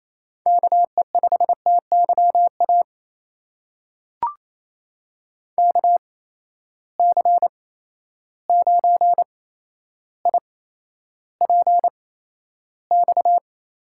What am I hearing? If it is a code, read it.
Morse code, heard as a keyed tone, about 28 wpm, spaced as for 12 wpm, 700 Hz. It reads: KE5TYA KC9IPX
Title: USA Call Signs - 6 Characters - 8x Character Spacing - 28wpm